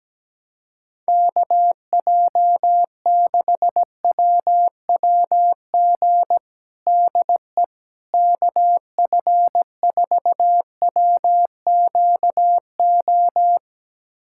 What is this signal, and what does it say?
Morse code, heard as a keyed tone, about 17 wpm, 700 Hz. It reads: KJ6WWG DE KF4WQO
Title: USA Callsign DE Callsign - 6 Characters - 17wpm